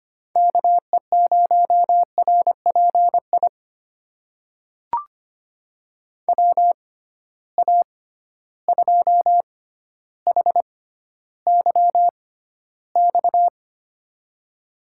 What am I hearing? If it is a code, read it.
Morse code, heard as a keyed tone, about 25 wpm, spaced as for 13 wpm, 700 Hz. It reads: KE0RPI WA2HYX
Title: USA Call Signs - 6 Characters - 6x Character Spacing - 25wpm